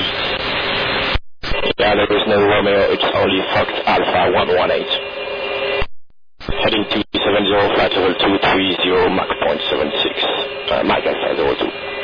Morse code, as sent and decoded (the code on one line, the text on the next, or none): none